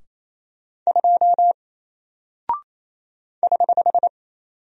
Morse code, heard as a keyed tone, about 27 wpm, 700 Hz. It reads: 2 <HH>
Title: Single Character (Letter:Number:Punctuation:Prosign) 28wpm